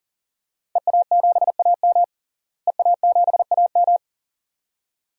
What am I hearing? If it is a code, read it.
Morse code, heard as a keyed tone, about 40 wpm, 700 Hz. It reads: EA7AM EA7AM
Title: DXCC-100 Call Signs - 5 Characters - Repeated 2x - 40wpm